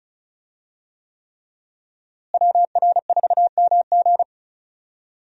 Morse code, heard as a keyed tone, about 36 wpm, 700 Hz. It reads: WR4MG
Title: USA Call Signs - 5 Characters - Repeated 3x - 35wpm